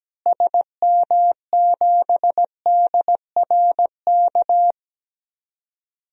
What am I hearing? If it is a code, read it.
Morse code, heard as a keyed tone, about 17 wpm, 700 Hz. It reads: SM7DRK
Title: DXCC-100 Call Signs - 6 Characters - Repeated 2x - 17wpm